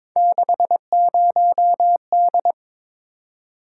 Morse code, heard as a keyed tone, about 22 wpm, 700 Hz. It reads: 60D
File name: practice / 22wpm / Three Letter-Number Combinations - 22wpm